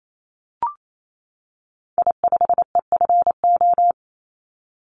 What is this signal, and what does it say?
Morse code, heard as a keyed tone, about 28 wpm, 700 Hz. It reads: I5EFO
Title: Call Signs - 28wpm